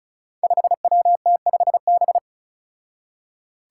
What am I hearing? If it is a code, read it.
Morse code, heard as a keyed tone, about 36 wpm, 700 Hz. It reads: 5WT5B
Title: Five Challenging Letter-Number Combinations 35wpm